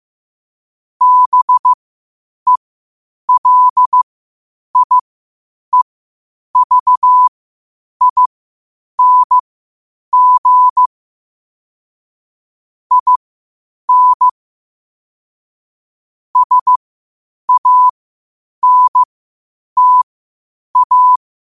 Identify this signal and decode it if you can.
Morse code, heard as a keyed tone, about 15 wpm, spaced as for 8 wpm, 1000 Hz. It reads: BELIEVING IN SANTA
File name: qotdfiles / 8wpm